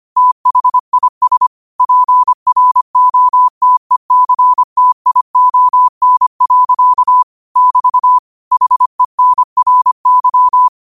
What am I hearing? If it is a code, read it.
Morse code, heard as a keyed tone, about 25 wpm, 1000 Hz. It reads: THIS PROTECTION. <BT> HENRY
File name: qotdfiles / 25wpm